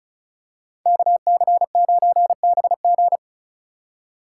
Morse code, heard as a keyed tone, about 36 wpm, 700 Hz. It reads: KC9BG
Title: K1USN SST QSO Practice - 35wpm